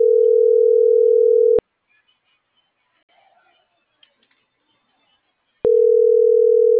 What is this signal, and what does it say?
Ringback tone (North America), 440 + 480 Hz.